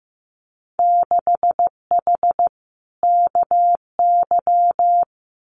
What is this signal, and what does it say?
Morse code, heard as a keyed tone, about 15 wpm, 700 Hz. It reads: 6H KY
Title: Winter Field Day Exchanges Encore 15wpm